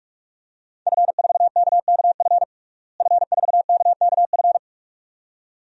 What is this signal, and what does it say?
Morse code, heard as a keyed tone, about 41 wpm, 700 Hz. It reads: F4KKF F4KKF
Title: DXCC-100 Call Signs - 5 Characters - Repeated 2x - 45wpm